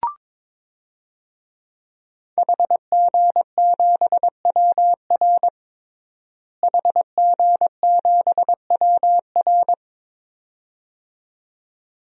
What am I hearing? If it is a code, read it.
Morse code, heard as a keyed tone, about 22 wpm, 700 Hz. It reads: HG7WR HG7WR
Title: DXCC-100 Call Signs - 5 Characters - Repeated 2x - 22wpm